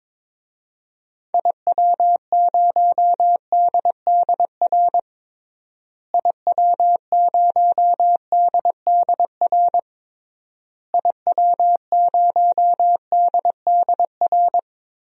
Morse code, heard as a keyed tone, about 22 wpm, 700 Hz. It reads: IW0DDR IW0DDR IW0DDR
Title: DXCC-100 Call Signs - 6 Characters - Repeated 3x - 22wpm